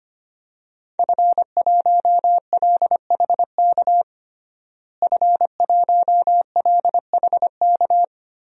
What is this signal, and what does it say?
Morse code, heard as a keyed tone, about 25 wpm, 700 Hz. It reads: F1LHK F1LHK
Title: DXCC-100 Call Signs - 5 Characters - Repeated 2x - 25wpm